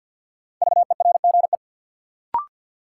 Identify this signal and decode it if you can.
Morse code, heard as a keyed tone, about 52 wpm, 700 Hz. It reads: VERGE
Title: Sets of 4 Words - New Word Drill - 50wpm